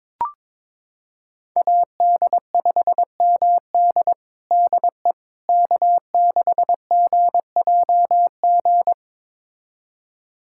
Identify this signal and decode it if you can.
Morse code, heard as a keyed tone, about 22 wpm, 700 Hz. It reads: AD5MD DE K6GJG
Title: USA Callsign DE Callsign - 5 Characters - 22wpm